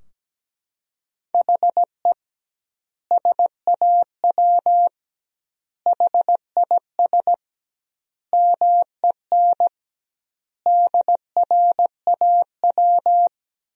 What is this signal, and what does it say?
Morse code, heard as a keyed tone, about 17 wpm, 700 Hz. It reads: HE SAW HIS MEN DRAW
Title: 2x Word-Spacing - Sentences from Top 300 Words - 17wpm